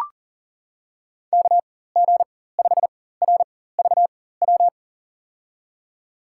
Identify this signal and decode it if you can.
Morse code, heard as a keyed tone, about 40 wpm, spaced as for 19 wpm, 700 Hz. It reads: KG5RVW
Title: USA Call Signs - 6 Characters - 4x Character Spacing - 40wpm